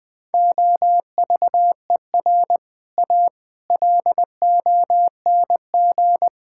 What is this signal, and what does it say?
Morse code, heard as a keyed tone, about 20 wpm, 700 Hz. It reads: OVER A LONG
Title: Sentences with 6-Word Phrases - 20wpm